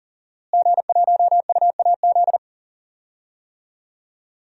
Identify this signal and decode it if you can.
Morse code, heard as a keyed tone, about 39 wpm, 700 Hz. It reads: G1UAZ
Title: DXCC-100 Call Signs - 5 Characters - Repeated 2x - 40wpm